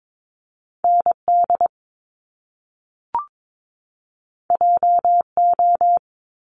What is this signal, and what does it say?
Morse code, heard as a keyed tone, about 22 wpm, 700 Hz. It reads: ND JO
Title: Top Two Letter Combinations in Words 22wpm